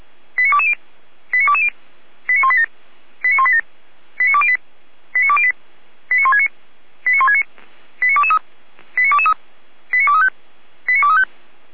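Five-tone selective call (ZVEI), heard as quick runs of five tones, tones of about 80 ms. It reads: 89220 89220 89187 89187 89208 89208 89169 89169 89223 89223 89235 89235
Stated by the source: ZVEI five-tone selective call received by SDR, fire brigade alerting, Feuerwehr Orlamünde.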